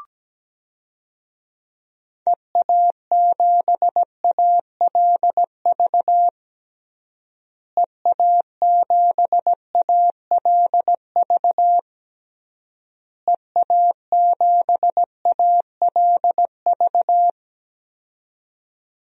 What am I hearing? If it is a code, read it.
Morse code, heard as a keyed tone, about 17 wpm, 700 Hz. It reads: EA7ALV EA7ALV EA7ALV
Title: DXCC-100 Call Signs - 6 Characters - Repeated 3x - 17wpm